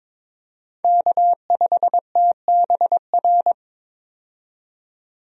Morse code, heard as a keyed tone, about 22 wpm, 700 Hz. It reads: K5TBR
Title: USA Call Signs - 5 Characters - 22wpm